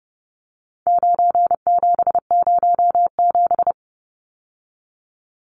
Morse code, heard as a keyed tone, about 30 wpm, 700 Hz. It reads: 9707
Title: Numbers - 4-Digit 30wpm